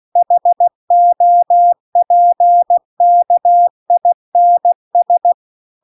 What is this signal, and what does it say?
Morse code, heard as a keyed tone, about 16 wpm, 700 Hz. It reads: HOPKINS